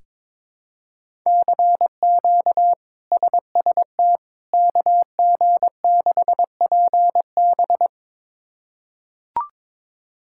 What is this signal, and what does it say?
Morse code, heard as a keyed tone, about 22 wpm, 700 Hz. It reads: CQ SST KG6PB
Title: K1USN SST QSO Practice - 22wpm